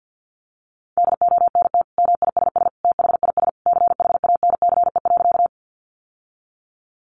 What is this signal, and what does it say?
Morse code, heard as a keyed tone, about 50 wpm, 700 Hz. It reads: DONT MISS THIS CHANCE.
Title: Sentences from Top 800 Words 50wpm